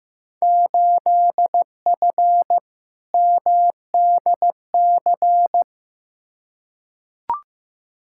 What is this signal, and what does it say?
Morse code, heard as a keyed tone, about 15 wpm, 700 Hz. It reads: 8F MDC
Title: Field Day Exchanges Encore - 15wpm